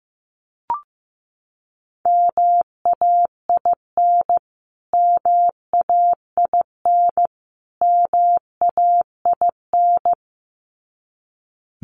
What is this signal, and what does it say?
Morse code, heard as a keyed tone, about 15 wpm, 700 Hz. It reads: MAIN MAIN MAIN
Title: Top 500 Words Repeated 15wpm